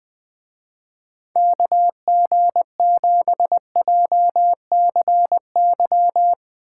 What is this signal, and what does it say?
Morse code, heard as a keyed tone, about 19 wpm, 700 Hz. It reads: KG7JCY